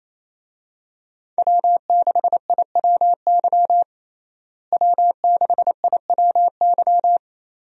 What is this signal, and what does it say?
Morse code, heard as a keyed tone, about 28 wpm, 700 Hz. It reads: W6IWY W6IWY
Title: USA Call Signs - 5 Characters - Repeated 2x - 28wpm